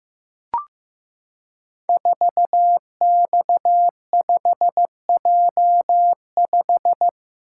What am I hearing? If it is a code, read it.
Morse code, heard as a keyed tone, about 15 wpm, 700 Hz. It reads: 4X5J5
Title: Code Groups - 5 Letter:Numbers - 15wpm